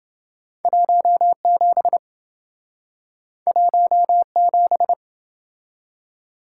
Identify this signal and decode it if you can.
Morse code, heard as a keyed tone, about 28 wpm, 700 Hz. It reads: 17 17